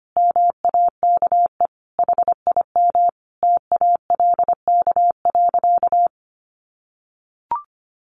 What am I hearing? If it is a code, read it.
Morse code, heard as a keyed tone, about 25 wpm, 700 Hz. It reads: MAKE HIM TALK.
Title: Sentences from Top 600 Words 25wpm